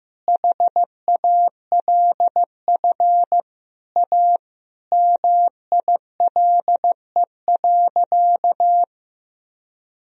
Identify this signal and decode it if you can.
Morse code, heard as a keyed tone, about 15 wpm, 700 Hz. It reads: HALF A MILE.
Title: Sentences from Top 1000 Words - Review Part 1 - 15wpm